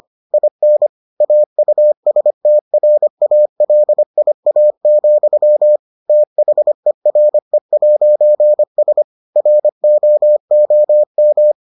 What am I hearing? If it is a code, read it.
Morse code, heard as a keyed tone, about 25 wpm, 600 Hz. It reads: IN AUSTRALIA, THERE'S ROOM